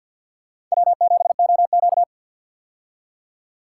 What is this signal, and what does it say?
Morse code, heard as a keyed tone, about 46 wpm, 700 Hz. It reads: W7OQ
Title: K1USN SST QSO Practice - 50wpm